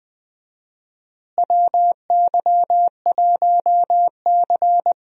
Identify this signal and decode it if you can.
Morse code, heard as a keyed tone, about 20 wpm, 700 Hz. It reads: WY1C
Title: USA Call Signs - 4 Characters - Repeated 3x - 20wpm